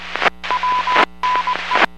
Morse code, heard as a keyed tone, about 20 wpm, 1000 Hz, with noise. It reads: WNE